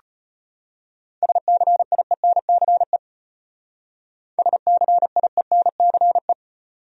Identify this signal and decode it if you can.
Morse code, heard as a keyed tone, about 31 wpm, 700 Hz. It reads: SCIENCE SCIENCE
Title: Speed-Racing Top 500 Words 25wpm